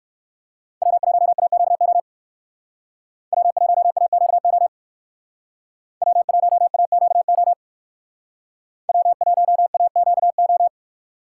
Morse code, heard as a keyed tone, about 58 wpm, 700 Hz. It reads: W1AQO W1AQO W1AQO W1AQO